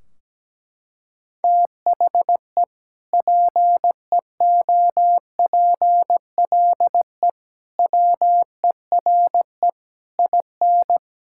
Morse code, heard as a keyed tone, about 17 wpm, 700 Hz. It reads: THE PEOPLE WERE IN
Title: Sentences from Top 1600 Words 17wpm